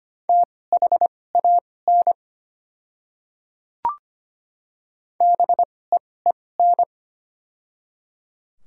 Morse code, heard as a keyed tone, about 25 wpm, spaced as for 18 wpm, 700 Hz. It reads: THAN BEEN
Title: 2x Character Spacing - Top 100 Words - 25wpm